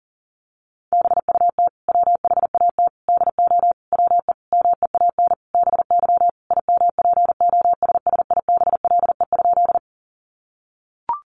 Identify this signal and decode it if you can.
Morse code, heard as a keyed tone, about 40 wpm, 700 Hz. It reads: BUT WHAT DO WE MEAN BY IMPOSSIBLE?
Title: Sentences from Top 1100 Words 40wpm